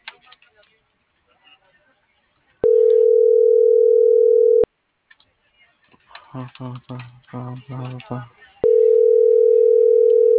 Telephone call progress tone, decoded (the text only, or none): ringback tone